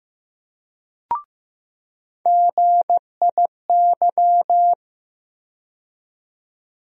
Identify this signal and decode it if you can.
Morse code, heard as a keyed tone, about 15 wpm, 700 Hz. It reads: GIY
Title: USA Call Sign Postfixes - 3 Letters 15wpm